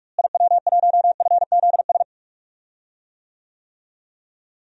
Morse code, heard as a keyed tone, about 41 wpm, 700 Hz. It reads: IW1FZS
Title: DXCC-100 Call Signs - 6 Characters - Repeated 2x - 45wpm